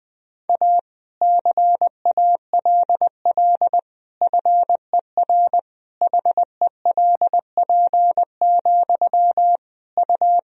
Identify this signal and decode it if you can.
Morse code, heard as a keyed tone, about 21 wpm, 700 Hz. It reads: A CALL FER HELP, U
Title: Sentences from Top 300 Words and Abbreviations 20wpm